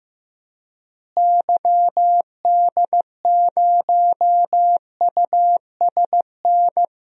Morse code, heard as a keyed tone, about 15 wpm, 700 Hz. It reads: YD0USN
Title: DXCC-100 Call Signs - 6 Characters - Repeated 3x - 15wpm